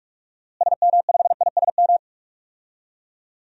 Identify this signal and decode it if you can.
Morse code, heard as a keyed tone, about 46 wpm, 700 Hz. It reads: SM5ISM
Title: DXCC-100 Call Signs - 6 Characters - Repeated 3x - 45wpm